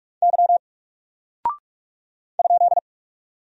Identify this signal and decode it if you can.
Morse code, heard as a keyed tone, about 44 wpm, 700 Hz. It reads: Y ?
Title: Single Character (Letter:Number:Punctuation:Prosign) 45wpm